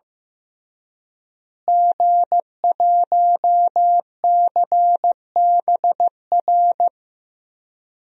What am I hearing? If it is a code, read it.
Morse code, heard as a keyed tone, about 15 wpm, 700 Hz. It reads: G1CBR
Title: DXCC-100 Call Signs - 5 Characters - Repeated 3x - 15wpm